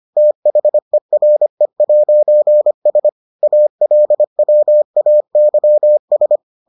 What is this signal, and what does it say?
Morse code, heard as a keyed tone, about 25 wpm, 600 Hz. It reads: THERE'S ALWAYS